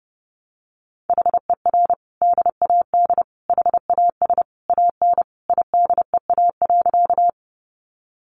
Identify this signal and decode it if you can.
Morse code, heard as a keyed tone, about 29 wpm, 700 Hz. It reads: HER DAD HAS AN IDEA.
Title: Easy Phrases 30wpm